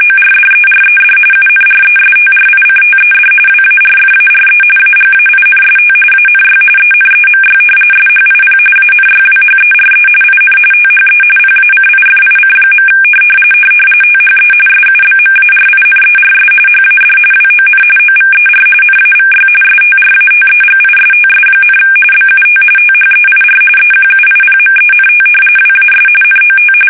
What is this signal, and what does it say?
RTTY, about 45.45 baud. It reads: ORDERED UNRESTRICTED WARFARE AGAINST ANY ENEMY SHIP ANYWHERE IN THE PACIFIC. THE SUBMARINERS HAD WON OVER NAVY STRATEGISTS, BUT THEIR EQUIPMENT WAS NOT
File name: fsk850